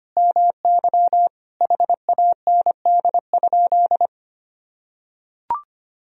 Morse code, heard as a keyed tone, about 25 wpm, 700 Hz. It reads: MY HAND?
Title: Sentences from Top 1000 Words - Review Part 4 - 25wpm